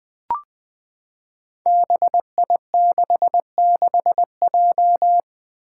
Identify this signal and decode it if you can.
Morse code, heard as a keyed tone, about 20 wpm, 700 Hz. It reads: BI66J
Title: Five Challenging Letter-Number Combinations 20wpm